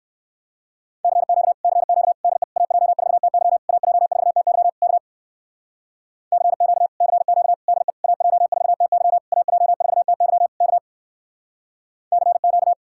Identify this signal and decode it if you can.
Morse code, heard as a keyed tone, about 60 wpm, 700 Hz. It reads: CQ CQ DE AJ4TY AJ4TY K CQ CQ DE AJ4TY AJ4TY K CQ